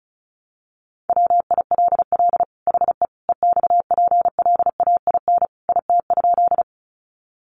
Morse code, heard as a keyed tone, about 34 wpm, 700 Hz. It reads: WILL HE EXPLAIN IT?
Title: Sentences from Top 1100 Words 35wpm